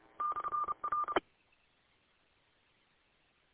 Morse code, heard as a keyed tone, about 30 wpm, 1200 Hz. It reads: /R